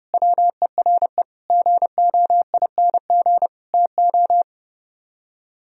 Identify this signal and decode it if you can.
Morse code, heard as a keyed tone, about 36 wpm, 700 Hz. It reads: WERE GOING TO
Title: Speed-Racing - Sets of 4 Words 30wpm